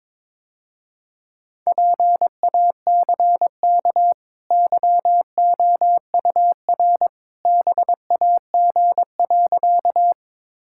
Morse code, heard as a keyed tone, about 21 wpm, 700 Hz. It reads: PACK YOUR BAG.
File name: practice / 22wpm / Easy Sentences with Hints - 22wpm